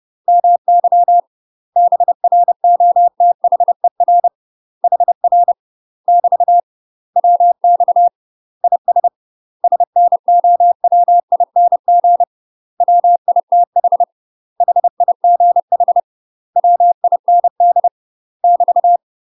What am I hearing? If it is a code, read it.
Morse code, heard as a keyed tone, about 30 wpm, 700 Hz. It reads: MY BROTHER HR <BT> WX IS SNOWING WITH HIGH WIND <BT>
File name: UK MASTER QSOs TRAINING / 30WPM / UK MASTER QSOs 0045